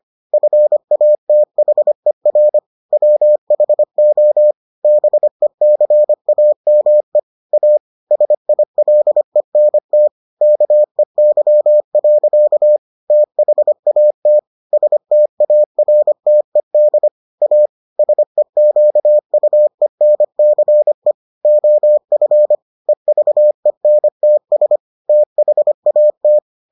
Morse code, heard as a keyed tone, about 25 wpm, 600 Hz. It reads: FATHER WHO BECAME A SILENT KEY. THAT STARTED A SEQUENCE OF EVENTS THAT